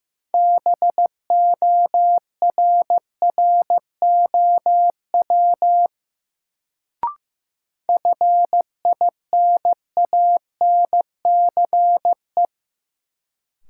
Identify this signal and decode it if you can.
Morse code, heard as a keyed tone, about 15 wpm, 700 Hz. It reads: BORROW FINANCE
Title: Sentences from Top 1500 Words - New Word Drill Alt - 15wpm